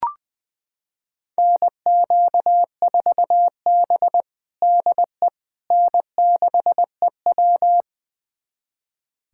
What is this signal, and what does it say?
Morse code, heard as a keyed tone, about 20 wpm, 700 Hz. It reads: NQ4B DE N6EW
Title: USA Callsign DE Callsign - 4 Characters - 20wpm